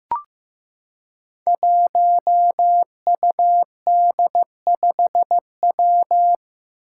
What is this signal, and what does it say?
Morse code, heard as a keyed tone, about 15 wpm, 700 Hz. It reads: 1UD5W
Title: Five Challenging Letter-Number Combinations 15wpm